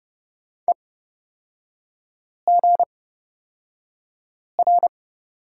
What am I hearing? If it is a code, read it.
Morse code, heard as a keyed tone, about 30 wpm, spaced as for 5 wpm, 700 Hz. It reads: EGR
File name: Lesson 38e - Letter q - Three Characters 30wpm